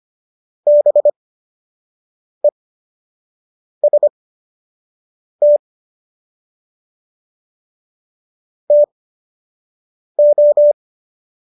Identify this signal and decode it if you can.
Morse code, heard as a keyed tone, about 25 wpm, spaced as for 6 wpm, 590 Hz. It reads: BEST TO